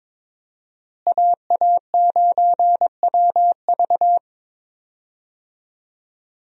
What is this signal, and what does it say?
Morse code, heard as a keyed tone, about 22 wpm, 700 Hz. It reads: AA9WV
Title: USA Call Signs - 5 Characters - Repeated 3x - 22wpm